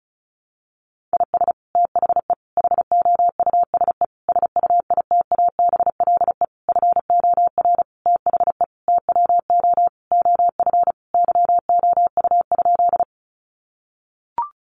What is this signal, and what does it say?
Morse code, heard as a keyed tone, about 34 wpm, 700 Hz. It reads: IS THE HOUSE SUITABLE FOR THE TWO OF YOU?
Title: Sentences from Top 1400 Words 35wpm